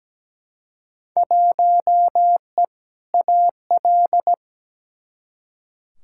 Morse code, heard as a keyed tone, about 17 wpm, 700 Hz. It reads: 1E AL
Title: Field Day Exchanges - 17wpm